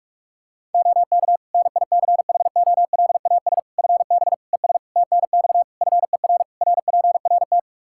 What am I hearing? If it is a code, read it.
Morse code, heard as a keyed tone, about 42 wpm, 700 Hz. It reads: OK NICHOLAS FB ES TNX FER RPRT
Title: Practice QSO - Exchange 2 - Info - 45wpm